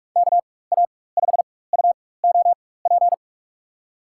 Morse code, heard as a keyed tone, about 46 wpm, spaced as for 21 wpm, 700 Hz. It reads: KA5UOP
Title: USA Call Signs - 6 Characters - 4x Character Spacing - 45wpm